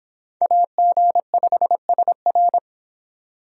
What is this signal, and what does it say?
Morse code, heard as a keyed tone, about 25 wpm, 700 Hz. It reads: AG5SR